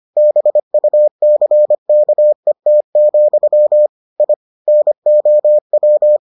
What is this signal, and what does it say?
Morse code, heard as a keyed tone, about 25 wpm, 600 Hz. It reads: BUCKET, I NOW